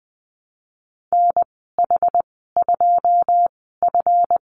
Morse code, heard as a keyed tone, about 20 wpm, spaced as for 15 wpm, 700 Hz. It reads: NH2F
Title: 173 Lesson 28g - Number 9 - Call Signs 20wpm